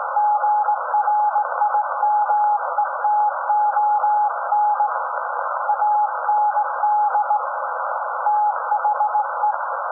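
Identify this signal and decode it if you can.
Morse code, heard as a keyed tone, about 18 wpm, 840 Hz. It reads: GRIMETON AND THE
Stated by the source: SAQ Grimeton Radio, Alexanderson 200 kW alternator on 17.2 kHz, received in Potsdam with a very strong signal on World Radio Day.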